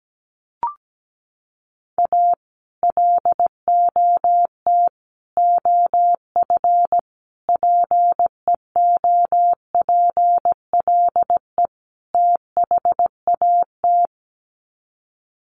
Morse code, heard as a keyed tone, about 17 wpm, 700 Hz. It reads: A LOT OF PEOPLE THAT
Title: Sets of 5 Words Encore2 17wpm